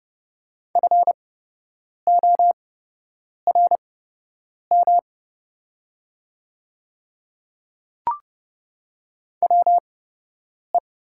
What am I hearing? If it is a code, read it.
Morse code, heard as a keyed tone, about 30 wpm, spaced as for 8 wpm, 700 Hz. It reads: FORM WE
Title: 8x Character Spacing - Top 200 Words - 30wpm